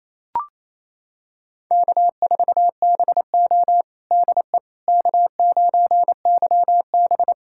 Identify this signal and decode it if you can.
Morse code, heard as a keyed tone, about 28 wpm, 700 Hz. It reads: K4BO DE K9YB
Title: USA Callsign DE Callsign - 4 Characters - 28wpm